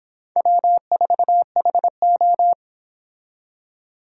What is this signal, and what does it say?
Morse code, heard as a keyed tone, about 25 wpm, 700 Hz. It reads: W4HO